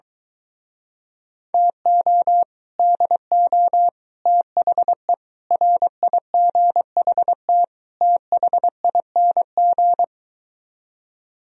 Morse code, heard as a keyed tone, about 23 wpm, 700 Hz. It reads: TO DO THE RIGHT THING